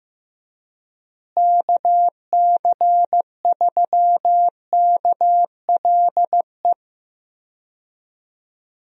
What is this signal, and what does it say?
Morse code, heard as a keyed tone, about 15 wpm, 700 Hz. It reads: KC3KLE
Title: USA Call Signs - 6 Characters - Repeated 3x - 15wpm